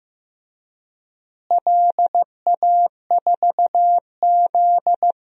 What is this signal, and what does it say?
Morse code, heard as a keyed tone, about 15 wpm, 700 Hz. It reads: LA4Z
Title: DXCC-100 Call Signs - 4 Characters - Repeated 2x - 15wpm